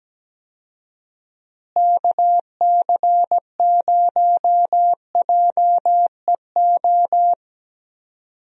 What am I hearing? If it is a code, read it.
Morse code, heard as a keyed tone, about 17 wpm, 700 Hz. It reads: KC0JEO
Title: USA Call Signs - 6 Characters - Repeated 3x - 17wpm